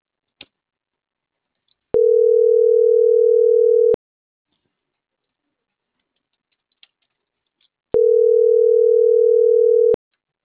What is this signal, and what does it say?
Ringback tone (North America), 440 + 480 Hz.